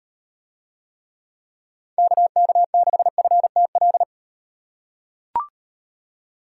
Morse code, heard as a keyed tone, about 36 wpm, 700 Hz. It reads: KK6FTL